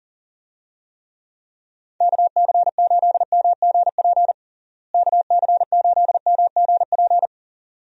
Morse code, heard as a keyed tone, about 40 wpm, 700 Hz. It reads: KC8MGP KC8MGP
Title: USA Call Signs - 6 Characters - Repeated 2x - 40wpm